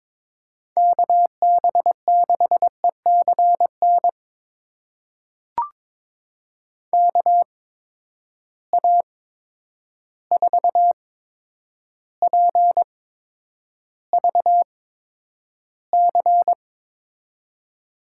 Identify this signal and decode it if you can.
Morse code, heard as a keyed tone, about 22 wpm, spaced as for 9 wpm, 700 Hz. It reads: KB6ECN KA4PVC
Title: USA Call Signs - 6 Characters - 8x Character Spacing - 22wpm